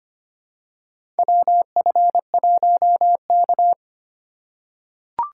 Morse code, heard as a keyed tone, about 25 wpm, 700 Hz. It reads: WF1K
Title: USA Call Signs - 4x Character Spacing - 25wpm